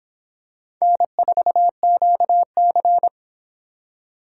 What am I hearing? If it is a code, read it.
Morse code, heard as a keyed tone, about 25 wpm, 700 Hz. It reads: N4QC